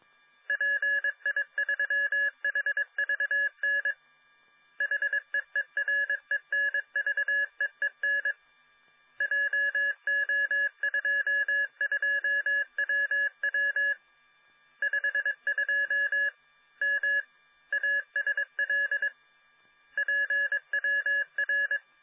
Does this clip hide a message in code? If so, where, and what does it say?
Yes: Morse code from 0.5 to 21.8 s, heard as a keyed tone, about 22 wpm, 1700 Hz. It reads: PI3HVN HEERENVEEN JO22WW 52 M ASL PWR